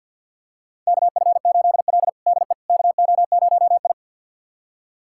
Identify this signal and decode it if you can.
Morse code, heard as a keyed tone, about 50 wpm, 700 Hz. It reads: KF8L DE KO0I
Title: USA Callsign DE Callsign - 4 Characters - 50wpm